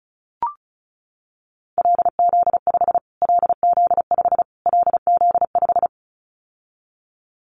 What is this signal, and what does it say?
Morse code, heard as a keyed tone, about 34 wpm, 700 Hz. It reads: LZ5 LZ5 LZ5
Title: Callsign Prefixes for Top 25 DXCC - Repeated 3x - 35wpm